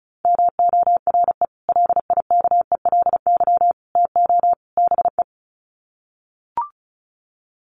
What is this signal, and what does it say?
Morse code, heard as a keyed tone, about 34 wpm, 700 Hz. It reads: MORE LIKELY TO BE
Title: Sets of 4 Words 35wpm